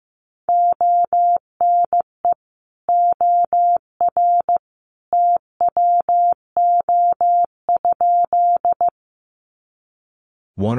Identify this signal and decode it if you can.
Morse code, heard as a keyed tone, about 15 wpm, 700 Hz. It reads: ONE OR TWO?
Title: Sentences from Top 100 Words 15wpm